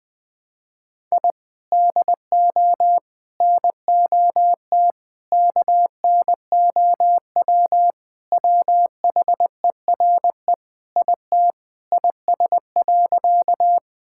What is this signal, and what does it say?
Morse code, heard as a keyed tone, about 20 wpm, 700 Hz. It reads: I DO NOT KNOW WHERE IT IS.
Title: Sentences with 5-Word Phrases - 20wpm